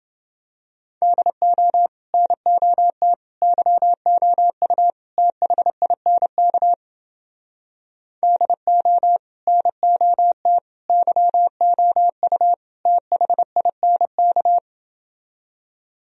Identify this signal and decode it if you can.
Morse code, heard as a keyed tone, about 29 wpm, 700 Hz. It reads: DO NOT YOU THINK DO NOT YOU THINK